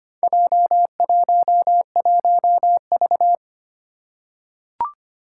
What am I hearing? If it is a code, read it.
Morse code, heard as a keyed tone, about 25 wpm, 700 Hz. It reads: J11V
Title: Four Challenging Letter-Number Combinations 25wpm